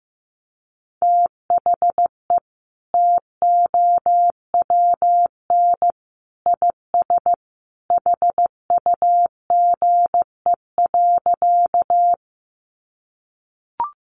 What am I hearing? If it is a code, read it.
Morse code, heard as a keyed tone, about 15 wpm, 700 Hz. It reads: THE TOWN IS HUGE.
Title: Sentences from Top 1100 Words 15wpm